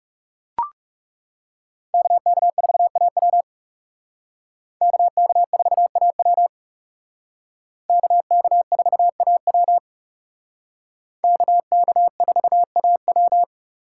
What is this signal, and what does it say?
Morse code, heard as a keyed tone, about 36 wpm, 700 Hz. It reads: KK4AW KK4AW KK4AW KK4AW
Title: Speed Racing - USA Call Signs 30wpm